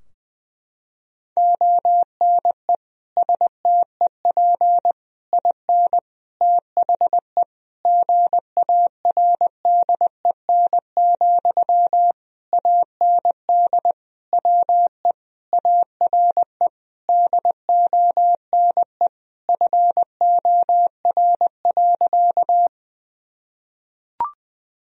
Morse code, heard as a keyed tone, about 20 wpm, 700 Hz. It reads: ONE STEP IN THE GARDEN, AND WE ARE DONE FOR.
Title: Sentences from Top 1000 Words - Review Part 4 - 20wpm